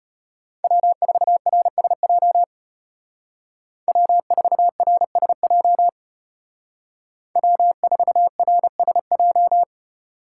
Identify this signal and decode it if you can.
Morse code, heard as a keyed tone, about 33 wpm, 700 Hz. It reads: W4RSJ W4RSJ W4RSJ